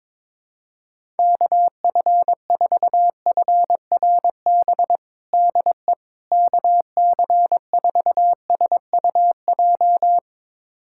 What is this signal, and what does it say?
Morse code, heard as a keyed tone, about 21 wpm, 700 Hz. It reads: KF4FRB DE KC4SUJ
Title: USA Callsign DE Callsign - 6 Characters - 22wpm